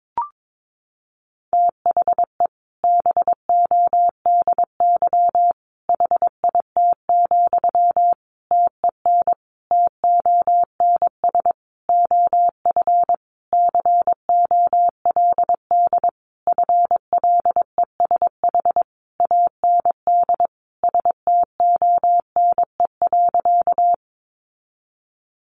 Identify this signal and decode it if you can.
Morse code, heard as a keyed tone, about 22 wpm, 700 Hz. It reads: THE BODY HIT, TEN TONS OF COLD FLESH AND STONE.
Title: A Sound Of Thunder 22wpm